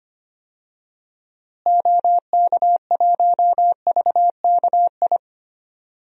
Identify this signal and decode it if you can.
Morse code, heard as a keyed tone, about 25 wpm, 700 Hz. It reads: OK1VKI
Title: DXCC-100 Call Signs - 6 Characters - Repeated 2x - 25wpm